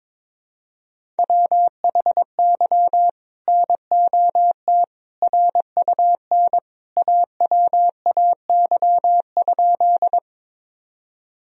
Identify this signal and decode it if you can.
Morse code, heard as a keyed tone, about 22 wpm, 700 Hz. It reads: WHY NOT RUN AWAY?
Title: Sentences from Top 1000 Words - Review Part 1 - 22wpm